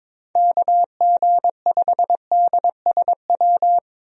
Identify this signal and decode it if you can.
Morse code, heard as a keyed tone, about 22 wpm, 700 Hz. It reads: KG5DSW
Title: USA Call Signs - 6 Characters - 22wpm